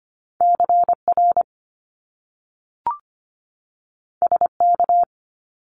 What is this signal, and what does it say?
Morse code, heard as a keyed tone, about 25 wpm, 700 Hz. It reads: CR SK
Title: Top Two Letter Combinations in Words 25wpm